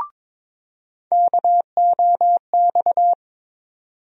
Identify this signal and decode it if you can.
Morse code, heard as a keyed tone, about 23 wpm, 700 Hz. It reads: KOX
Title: USA Call Sign Postfixes - 3 Letters 22wpm